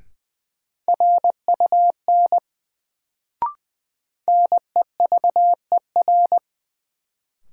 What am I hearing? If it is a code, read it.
Morse code, heard as a keyed tone, about 20 wpm, 700 Hz. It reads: RUN NEVER
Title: Sentences from Top 300 Words - New Word Drill - 20wpm